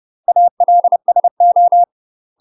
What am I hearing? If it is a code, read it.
Morse code, heard as a keyed tone, about 30 wpm, 700 Hz. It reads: ALSO
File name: UK MASTER QSOs TRAINING / 30WPM / UK MASTER QSOs 0030